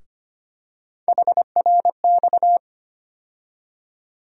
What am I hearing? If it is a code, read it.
Morse code, heard as a keyed tone, about 25 wpm, 700 Hz. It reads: HRX
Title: USA Call Sign Postfixes - 3 Letters 25wpm